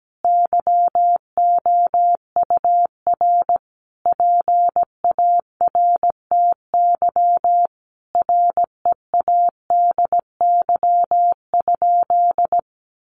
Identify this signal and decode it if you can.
Morse code, heard as a keyed tone, about 17 wpm, 700 Hz. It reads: YOUR PARTY READY?
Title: Sentences from Top 1400 Words 17wpm